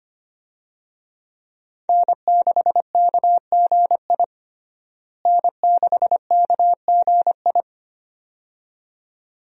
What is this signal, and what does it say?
Morse code, heard as a keyed tone, about 25 wpm, 700 Hz. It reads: N6KGI N6KGI
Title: USA Call Signs - 5 Characters - Repeated 2x - 25wpm